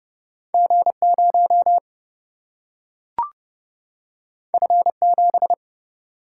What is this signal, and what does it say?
Morse code, heard as a keyed tone, about 30 wpm, 700 Hz. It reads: G 0 F 7
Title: Single-Letter Single-Number 30wpm